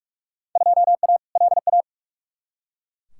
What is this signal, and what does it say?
Morse code, heard as a keyed tone, about 42 wpm, 700 Hz. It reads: 2A LA
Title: Field Day Exchanges Encore - 45wpm